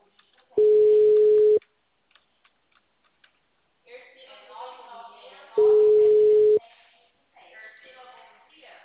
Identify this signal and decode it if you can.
Ringback tone (Europe, 425 Hz), with speech.